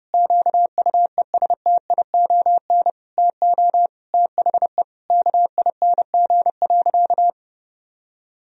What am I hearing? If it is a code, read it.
Morse code, heard as a keyed tone, about 30 wpm, 700 Hz. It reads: QUESTION TO THE KING.
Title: Sentences from Top 1000 Words - Review Part 2 - 30wpm